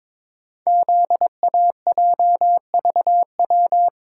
Morse code, heard as a keyed tone, about 23 wpm, 700 Hz. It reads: ZAJVW
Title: Code Groups - 5 Letters - 22wpm